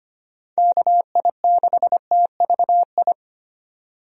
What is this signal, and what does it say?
Morse code, heard as a keyed tone, about 25 wpm, 700 Hz. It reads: KI6TVI
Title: USA Call Signs - 6 Characters - 25wpm